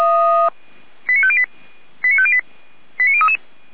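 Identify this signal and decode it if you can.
Five-tone selective call (ZVEI), heard as quick runs of five tones, tones of about 70 ms. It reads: 89508 89508 89033
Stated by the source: Five-tone ZVEI selective call received on SDR, from the feuerwehr-orlamuende.org fire brigade audio server (f85360).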